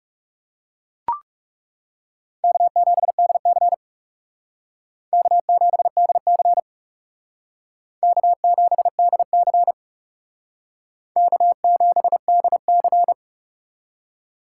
Morse code, heard as a keyed tone, about 36 wpm, 700 Hz. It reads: K7DC K7DC K7DC K7DC
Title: Speed Racing - USA 1x2 Call Signs 30wpm